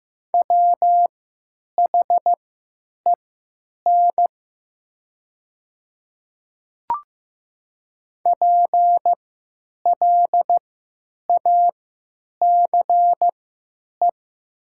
Morse code, heard as a keyed tone, about 15 wpm, spaced as for 9 wpm, 700 Hz. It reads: WHEN PLACE
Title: 3x Character Spacing - Top 300 Words - 15wpm